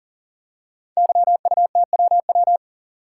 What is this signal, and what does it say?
Morse code, heard as a keyed tone, about 40 wpm, 700 Hz. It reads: YUTWW
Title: Code Groups - 5 Letters - 40wpm